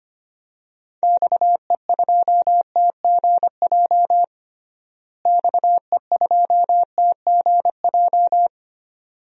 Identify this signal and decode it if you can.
Morse code, heard as a keyed tone, about 25 wpm, 700 Hz. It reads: XE2TGJ XE2TGJ
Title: DXCC-100 Call Signs - 6 Characters - Repeated 2x - 25wpm